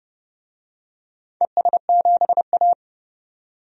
Morse code, heard as a keyed tone, about 29 wpm, 700 Hz. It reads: ES7A
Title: DXCC-100 Call Signs - 4 Characters - 30wpm